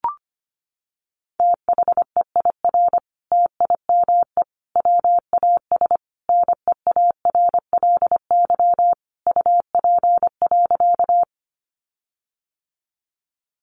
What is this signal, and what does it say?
Morse code, heard as a keyed tone, about 25 wpm, 700 Hz. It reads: THEIR TIME WAS NEARLY UP.
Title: Sentences from Top 1300 Words 25wpm